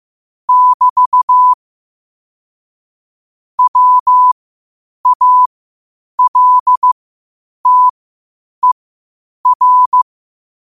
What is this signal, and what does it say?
Morse code, heard as a keyed tone, about 15 wpm, spaced as for 8 wpm, 1000 Hz. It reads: <BT> WALTER
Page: qotd-at-8-wpm